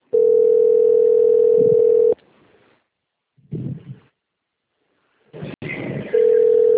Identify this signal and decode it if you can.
Ringback tone (North America), 440 + 480 Hz.
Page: Valinhos - SP